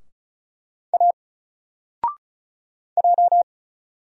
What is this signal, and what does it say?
Morse code, heard as a keyed tone, about 35 wpm, 700 Hz. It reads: A J
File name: Single-Letter-Number-35wpm